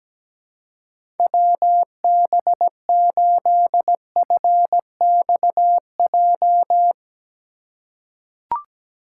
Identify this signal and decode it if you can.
Morse code, heard as a keyed tone, about 17 wpm, 700 Hz. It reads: WB8FXJ
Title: K1USN SST QSO Practice - 17wpm